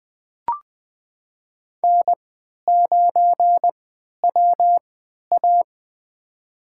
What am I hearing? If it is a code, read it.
Morse code, heard as a keyed tone, about 20 wpm, spaced as for 11 wpm, 700 Hz. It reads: N9WA
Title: USA Call Signs - 4 Characters - 3x Character Spacing - 20wpm